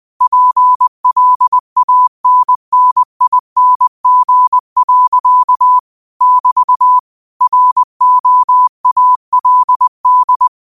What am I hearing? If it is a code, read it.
Morse code, heard as a keyed tone, about 20 wpm, 1000 Hz. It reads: PLANNING. <BT> ROALD
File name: qotdfiles / 20wpm